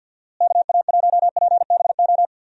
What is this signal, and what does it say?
Morse code, heard as a keyed tone, about 50 wpm, 700 Hz. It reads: KA1PBO
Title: USA Call Signs - 6 Characters - Repeated 2x - 50wpm